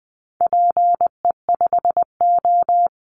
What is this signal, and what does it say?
Morse code, heard as a keyed tone, about 20 wpm, 700 Hz. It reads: PE5O
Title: Call Signs - 20wpm